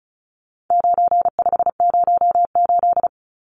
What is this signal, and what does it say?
Morse code, heard as a keyed tone, about 34 wpm, 700 Hz. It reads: 9508
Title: Numbers - 4-Digit 35wpm